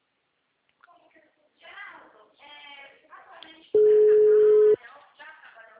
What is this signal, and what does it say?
Ringback tone (Europe, 425 Hz), with speech.
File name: out- -2008- - - .8004